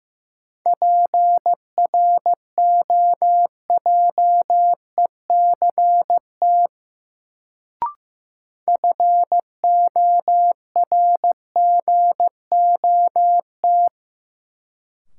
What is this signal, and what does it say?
Morse code, heard as a keyed tone, about 15 wpm, 700 Hz. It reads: PROJECT FORGOT
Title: Sentences from Top 1400 Words - New Word Drill Alt - 15wpm